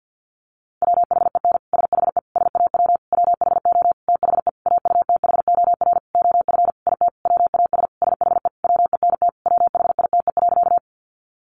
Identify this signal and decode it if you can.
Morse code, heard as a keyed tone, about 50 wpm, 700 Hz. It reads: WHEN SHE SAW WHO THE AUTHOR OF IT WAS SHE WENT WHITE.
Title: Sentences from Top 1200 Words 50wpm